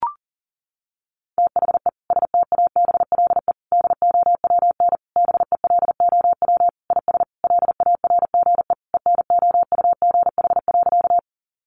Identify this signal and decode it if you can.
Morse code, heard as a keyed tone, about 40 wpm, 700 Hz. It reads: THE STABLE DOWN BELOW IS LARGE ENOUGH.
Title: Sentences from Top 1500 Words 40wpm